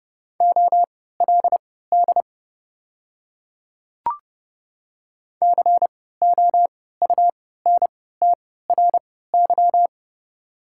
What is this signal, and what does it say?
Morse code, heard as a keyed tone, about 30 wpm, spaced as for 17 wpm, 700 Hz. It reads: OLD COUNTRY
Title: 3x Character Spacing - Top 300 Words - 30wpm